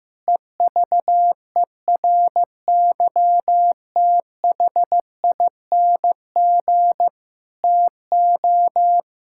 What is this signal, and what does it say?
Morse code, heard as a keyed tone, about 15 wpm, 700 Hz. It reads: EVERYTHING TO
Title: Sentences from Top 1000 Words - Review Part 6 - 15wpm